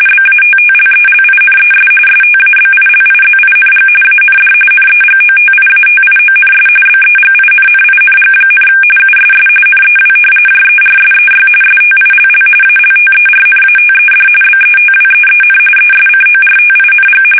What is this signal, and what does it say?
RTTY, about 45.45 baud. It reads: BATTLELINES STABILIZING AND NEGOTIATIONS BEGINNING, USS NEW JERSEY STILL ALTERNATED BETWEEN ESCORTING
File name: fsk850 / battleshipnj